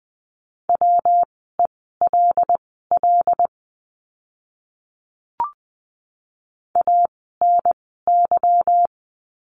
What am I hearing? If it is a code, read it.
Morse code, heard as a keyed tone, about 20 wpm, spaced as for 15 wpm, 700 Hz. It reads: WELL ANY
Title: 244 Lesson 39f - Letter z - Words 20wpm